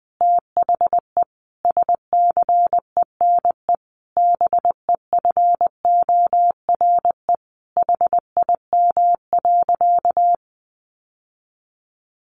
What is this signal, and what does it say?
Morse code, heard as a keyed tone, about 20 wpm, 700 Hz. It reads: THE SCENE BEFORE HIM.
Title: Sentences from Top 1100 Words 20wpm